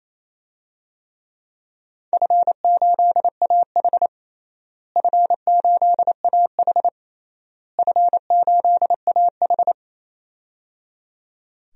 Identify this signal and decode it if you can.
Morse code, heard as a keyed tone, about 28 wpm, 700 Hz. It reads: F8AH F8AH F8AH
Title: DXCC-100 Call Signs - 4 Characters - Repeated 3x - 28wpm